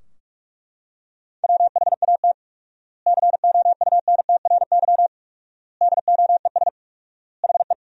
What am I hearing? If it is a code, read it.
Morse code, heard as a keyed tone, about 42 wpm, 700 Hz. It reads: WHAT COUNTRY DOES HE
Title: 4x Word-Spacing - Sentences from Top 300 Words - 45wpm